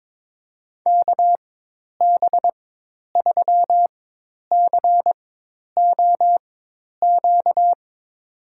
Morse code, heard as a keyed tone, about 22 wpm, spaced as for 10 wpm, 700 Hz. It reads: KB3COQ
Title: USA Call Signs - 6 Characters - 4x Character Spacing - 22wpm